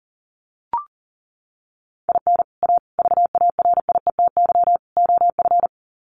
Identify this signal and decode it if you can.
Morse code, heard as a keyed tone, about 40 wpm, 700 Hz. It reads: IN A VARIETY OF
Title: Sets of 4 Words 40wpm